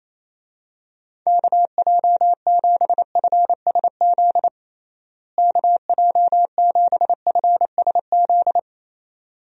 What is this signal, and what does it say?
Morse code, heard as a keyed tone, about 28 wpm, 700 Hz. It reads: KJ7FSZ KJ7FSZ
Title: USA Call Signs - 6 Characters - Repeated 2x - 28wpm